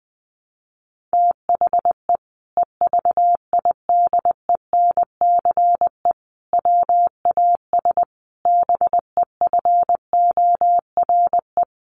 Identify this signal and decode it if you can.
Morse code, heard as a keyed tone, about 20 wpm, 700 Hz. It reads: THE EVIDENCE WAS BEFORE
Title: Sentences from Top 1300 Words 20wpm